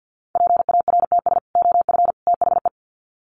Morse code, heard as a keyed tone, about 49 wpm, 700 Hz. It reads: PARTS OF THE
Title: Sets of 3 Words 50wpm